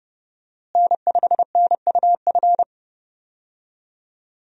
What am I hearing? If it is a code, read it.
Morse code, heard as a keyed tone, about 32 wpm, 700 Hz. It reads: N5NUF
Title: USA Call Signs - 5 Characters - Repeated 3x - 30wpm